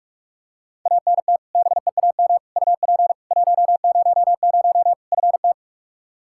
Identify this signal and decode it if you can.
Morse code, heard as a keyed tone, about 44 wpm, 700 Hz. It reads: ANT BEAM UP 100 FT
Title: K1USN SST QSO Practice 45wpm